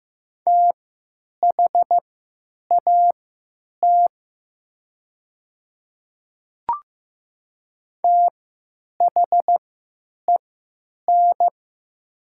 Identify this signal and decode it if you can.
Morse code, heard as a keyed tone, about 15 wpm, spaced as for 9 wpm, 700 Hz. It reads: THAT THEN
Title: 3x Character Spacing - Top 100 Words - 15wpm